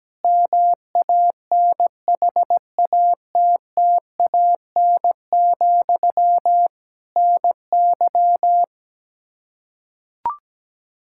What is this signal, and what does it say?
Morse code, heard as a keyed tone, about 17 wpm, 700 Hz. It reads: MANHATTAN, NY
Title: Practice QSO - Exchange 1 - Beginning - 17wpm